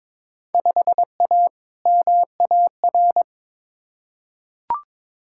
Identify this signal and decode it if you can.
Morse code, heard as a keyed tone, about 22 wpm, 700 Hz. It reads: 5A MAR
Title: Field Day Exchanges Encore - 22wpm